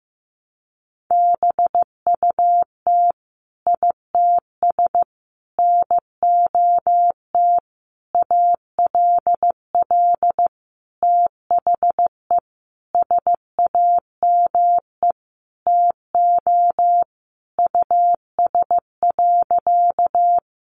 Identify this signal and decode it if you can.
Morse code, heard as a keyed tone, about 15 wpm, 700 Hz. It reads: BUT ITS NOT ALL THE SAME TO US.
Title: Sentences from Top 600 Words 15wpm